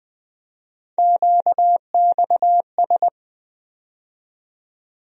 Morse code, heard as a keyed tone, about 20 wpm, 700 Hz. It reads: QXS
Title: USA Call Sign Postfixes - 3 Letters 20wpm